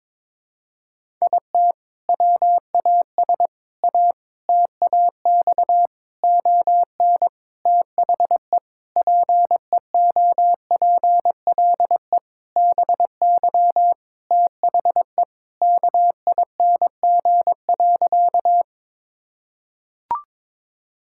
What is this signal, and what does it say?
Morse code, heard as a keyed tone, about 22 wpm, 700 Hz. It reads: IT WAS A TAX ON THE PEOPLE BY THE KING.
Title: Sentences from Top 1000 Words - Review Part 5 - 22wpm